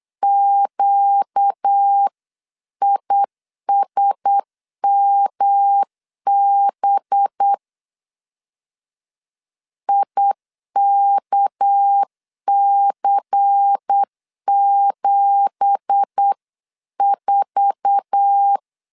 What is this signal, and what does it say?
Morse code, heard as a keyed tone, about 8 wpm, 800 Hz. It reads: QISMB IKC74
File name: LEZIONI / 9 2867 8wpm 3440